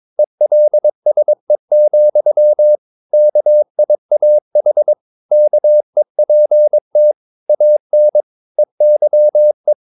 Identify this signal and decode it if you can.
Morse code, heard as a keyed tone, about 22 wpm, 600 Hz. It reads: ELSE, KIAH KEPT AN EYE
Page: Romanen Never af Ken Follet på engelsk.